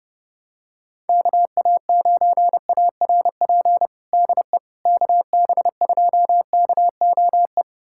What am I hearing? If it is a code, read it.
Morse code, heard as a keyed tone, about 32 wpm, 700 Hz. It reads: KA9ARP DE KB2KOE
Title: USA Callsign DE Callsign - 6 Characters - 30wpm